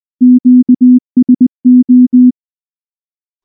Morse code, heard as a keyed tone, about 19 wpm, 250 Hz. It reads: QSO
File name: morse / morse-training-20WPM-4